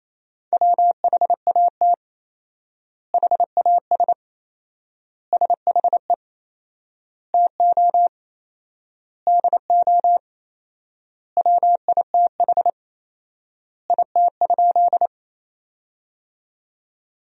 Morse code, heard as a keyed tone, about 28 wpm, 700 Hz. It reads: WHAT HAS SHE TO DO WITH IT?
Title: 4x Word-Spacing - Sentences from Top 100 Words - 28wpm